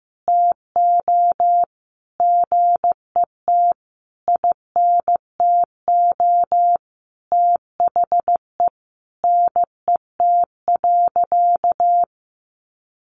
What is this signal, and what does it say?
Morse code, heard as a keyed tone, about 15 wpm, 700 Hz. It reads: TO GET INTO THE NET.
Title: Sentences from Top 1400 Words 15wpm